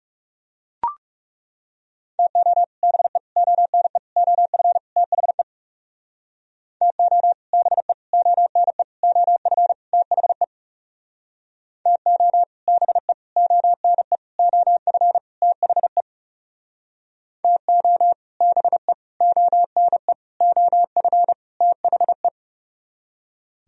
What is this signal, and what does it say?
Morse code, heard as a keyed tone, about 36 wpm, 700 Hz. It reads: TO BE ONE OF THE TO BE ONE OF THE TO BE ONE OF THE TO BE ONE OF THE
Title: Speed-Racing - Sets of 5 Words 30wpm